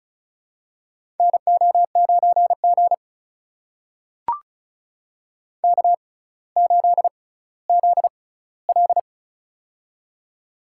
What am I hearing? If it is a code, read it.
Morse code, heard as a keyed tone, about 36 wpm, spaced as for 18 wpm, 700 Hz. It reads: NO9G K8ZL
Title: USA Call Signs - 4 Characters - 6x Character Spacing - 35wpm